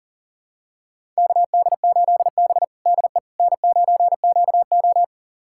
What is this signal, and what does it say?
Morse code, heard as a keyed tone, about 40 wpm, 700 Hz. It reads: KD8B DE N9QO
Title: USA Callsign DE Callsign - 4 Characters - 40wpm